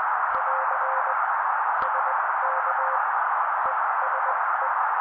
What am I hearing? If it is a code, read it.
Morse code, heard as a keyed tone, about 20 wpm, spaced as for 16 wpm, 530 Hz. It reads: <AR> SK ESE
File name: daily / 16wpm noise